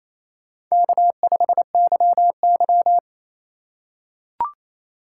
Morse code, heard as a keyed tone, about 28 wpm, 700 Hz. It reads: K5YY
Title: USA Call Signs - 4 Characters - 6x Character Spacing - 28wpm